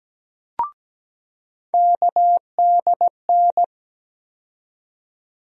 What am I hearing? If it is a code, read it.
Morse code, heard as a keyed tone, about 17 wpm, 700 Hz. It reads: KDN
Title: USA Call Sign Postfixes - 3 Letters 17wpm